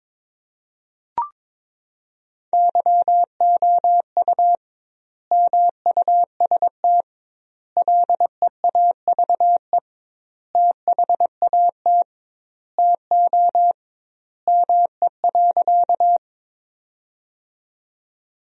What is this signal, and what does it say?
Morse code, heard as a keyed tone, about 22 wpm, 700 Hz. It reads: YOU MUST LEAVE THAT TO ME.
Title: 2x Word-Spacing - Sentences from Top 400 Words - 22wpm